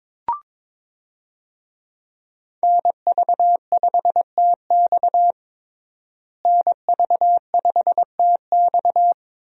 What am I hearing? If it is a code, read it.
Morse code, heard as a keyed tone, about 22 wpm, 700 Hz. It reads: NV5TX NV5TX
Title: USA Call Signs - 5 Characters - Repeated 2x - 22wpm